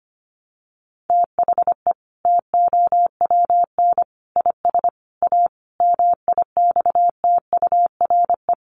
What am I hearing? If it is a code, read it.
Morse code, heard as a keyed tone, about 25 wpm, 700 Hz. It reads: THE TOWN IS A MIXTURE
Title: Sentences from Top 1500 Words 25wpm